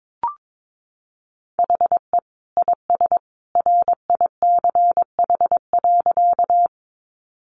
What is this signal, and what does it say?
Morse code, heard as a keyed tone, about 22 wpm, 700 Hz. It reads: HE IS RICH.
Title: Sentences from Top 1100 Words 22wpm